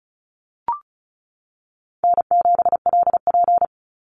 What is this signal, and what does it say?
Morse code, heard as a keyed tone, about 34 wpm, 700 Hz. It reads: N7LP
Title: NA SOTA Chaser Callsigns 35wpm